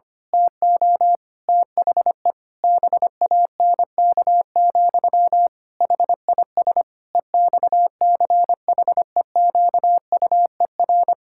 Morse code, heard as a keyed tone, about 25 wpm, 700 Hz. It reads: TO THE BANK, HIS EXCHEQUER
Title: Pangram Challenge v2 - 25wpm